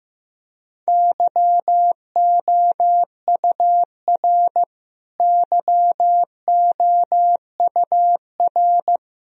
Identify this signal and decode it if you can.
Morse code, heard as a keyed tone, about 16 wpm, 700 Hz. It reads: YOUR YOUR
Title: Top 100 Words Repeated 15wpm